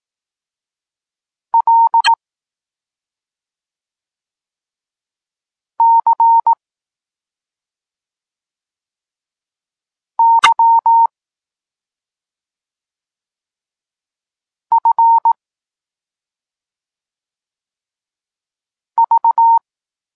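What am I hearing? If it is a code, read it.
Morse code, heard as a keyed tone, about 18 wpm, 920 Hz. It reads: L C Y F V
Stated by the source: Polish Morse lesson teaching the letter C, from SP9MOA Niepołomicki Klub Krótkofalowców.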